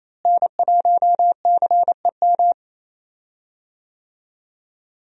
Morse code, heard as a keyed tone, about 28 wpm, 700 Hz. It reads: N1CEM
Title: USA Call Signs - 5 Characters - Repeated 2x - 28wpm